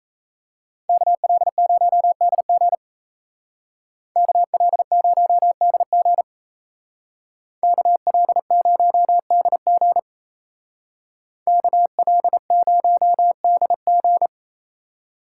Morse code, heard as a keyed tone, about 34 wpm, 700 Hz. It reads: KL0DG KL0DG KL0DG KL0DG
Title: Speed Racing - USA 2x2 Call Signs 28wpm